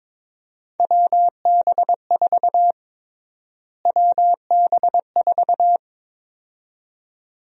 Morse code, heard as a keyed tone, about 22 wpm, 700 Hz. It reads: WB4 WB4
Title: USA Call Sign Prefixes - Repeated 2x - 22wpm